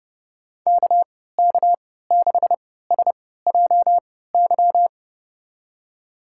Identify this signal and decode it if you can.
Morse code, heard as a keyed tone, about 32 wpm, spaced as for 17 wpm, 700 Hz. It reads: KK6SJY
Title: USA Call Signs - 3x Character Spacing - 30wpm